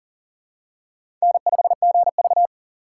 Morse code, heard as a keyed tone, about 37 wpm, 700 Hz. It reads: N5GV
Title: USA Call Signs - 4 Characters - 3x Character Spacing - 40wpm